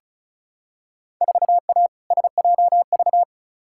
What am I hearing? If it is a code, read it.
Morse code, heard as a keyed tone, about 36 wpm, 700 Hz. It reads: 4A SJV
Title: Field Day Exchanges Encore - 35wpm